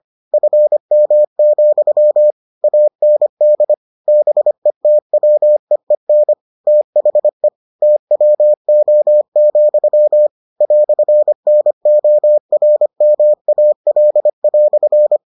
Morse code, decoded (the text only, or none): FM, AND BETWEEN THE TWO, "NORMAL"